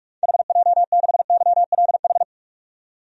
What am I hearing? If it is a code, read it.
Morse code, heard as a keyed tone, about 44 wpm, 700 Hz. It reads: HJ6YLH
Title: DXCC-100 Call Signs - 6 Characters - 45wpm